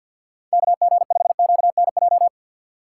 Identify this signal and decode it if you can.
Morse code, heard as a keyed tone, about 52 wpm, 700 Hz. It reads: KG5QNJ
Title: USA Call Signs - 3x Character Spacing - 50wpm